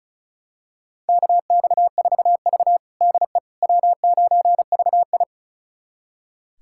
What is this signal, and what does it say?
Morse code, heard as a keyed tone, about 34 wpm, 700 Hz. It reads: KX4V DE W9VI
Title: USA Callsign DE Callsign - 4 Characters - 35wpm